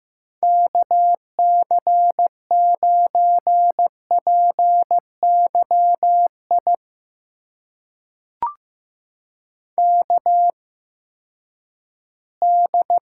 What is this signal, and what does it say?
Morse code, heard as a keyed tone, about 15 wpm, spaced as for 10 wpm, 700 Hz. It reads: KC9PYI KD
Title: USA Call Signs - 6 Characters - 8x Character Spacing - 15wpm